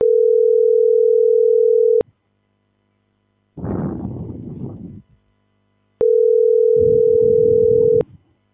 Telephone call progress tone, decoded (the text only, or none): ringback tone